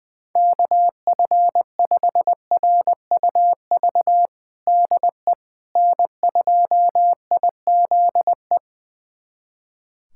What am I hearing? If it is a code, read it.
Morse code, heard as a keyed tone, about 20 wpm, 700 Hz. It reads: KF5RUV DE N2IZE
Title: USA Callsign DE Callsign 20wpm